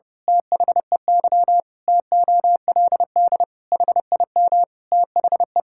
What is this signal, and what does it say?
Morse code, heard as a keyed tone, about 30 wpm, 700 Hz. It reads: THEY TOLD HIM THE
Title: Pangram Challenge v2 - 30wpm